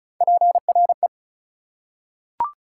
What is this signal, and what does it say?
Morse code, heard as a keyed tone, about 36 wpm, 700 Hz. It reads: PRE
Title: Prefixes and Suffixes 35wpm